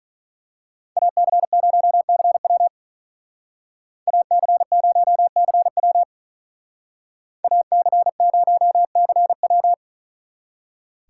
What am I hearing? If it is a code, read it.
Morse code, heard as a keyed tone, about 43 wpm, 700 Hz. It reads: AC0CW AC0CW AC0CW
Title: Speed Racing - USA 2x2 Call Signs 35wpm